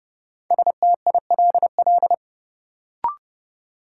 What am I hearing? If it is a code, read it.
Morse code, heard as a keyed tone, about 30 wpm, 700 Hz. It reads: STILL
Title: Sentences from Top 300 Words - New Word Drill - 30wpm